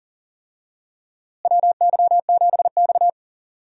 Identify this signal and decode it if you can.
Morse code, heard as a keyed tone, about 40 wpm, 700 Hz. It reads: WY7X
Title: USA Call Signs - 4 Characters - Repeated 3x - 40wpm